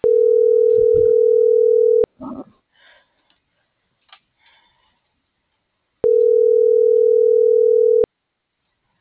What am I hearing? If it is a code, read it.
Ringback tone (North America), 440 + 480 Hz.